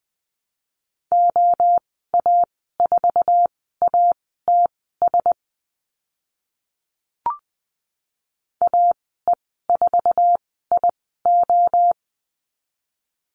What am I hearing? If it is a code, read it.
Morse code, heard as a keyed tone, about 20 wpm, spaced as for 14 wpm, 700 Hz. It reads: OA4ATS AE4IO
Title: 055 Lesson 10g - Letter r - Call Signs 20wpm